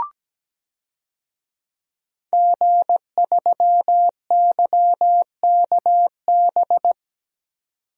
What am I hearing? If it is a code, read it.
Morse code, heard as a keyed tone, about 17 wpm, 700 Hz. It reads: G3YKB
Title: DXCC-100 Call Signs - 5 Characters - 17wpm